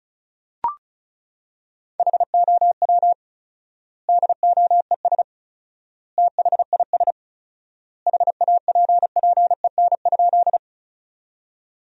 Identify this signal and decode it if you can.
Morse code, heard as a keyed tone, about 36 wpm, 700 Hz. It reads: HOW DOES THIS HAPPEN?
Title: 4x Word-Spacing - Sentences from Top 400 Words - 35wpm